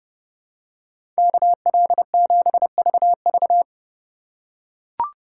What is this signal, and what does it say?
Morse code, heard as a keyed tone, about 30 wpm, 700 Hz. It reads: KL7VV
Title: USA Call Signs - 5 Characters - 4x Character Spacing - 30wpm